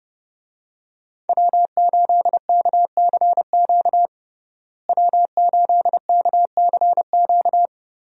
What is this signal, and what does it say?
Morse code, heard as a keyed tone, about 32 wpm, 700 Hz. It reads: W8KCQ W8KCQ
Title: USA Call Signs - 5 Characters - Repeated 2x - 30wpm